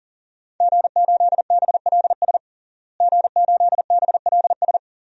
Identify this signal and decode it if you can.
Morse code, heard as a keyed tone, about 40 wpm, 700 Hz. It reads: G8BLS G8BLS
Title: DXCC-100 Call Signs - 5 Characters - Repeated 2x - 40wpm